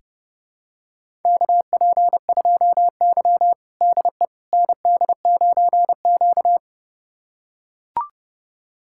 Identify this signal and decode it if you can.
Morse code, heard as a keyed tone, about 30 wpm, 700 Hz. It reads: KP2Y DE ND9Q
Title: USA Callsign DE Callsign - 4 Characters - 30wpm